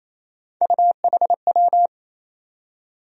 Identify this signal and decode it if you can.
Morse code, heard as a keyed tone, about 28 wpm, 700 Hz. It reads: UHW
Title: USA Call Sign Postfixes - 3 Letters 28wpm